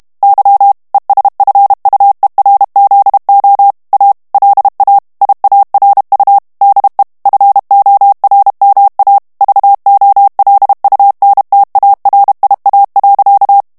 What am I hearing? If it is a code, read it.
Morse code, heard as a keyed tone, about 32 wpm, 800 Hz. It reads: Y ESFUERZO A LA IARU DE FORMA VOLUNTARIA.
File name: morse mp3 / ESPAGNOL TEXTE EN CLAIR 1 VITESSE 30 MOTS MINUTES